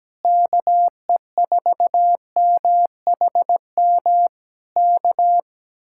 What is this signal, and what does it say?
Morse code, heard as a keyed tone, about 17 wpm, 700 Hz. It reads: KE4MHM K
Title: Speed-Racing Common QSO Phrases - 17wpm